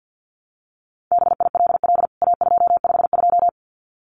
Morse code, heard as a keyed tone, about 50 wpm, 700 Hz. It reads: BILL A252
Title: CWT Exchange 50wpm